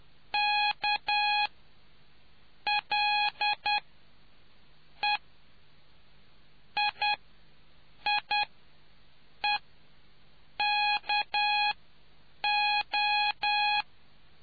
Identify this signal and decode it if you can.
Morse code, heard as a keyed tone, about 10 wpm, spaced as for 7 wpm, 830 Hz. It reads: K L E II E KO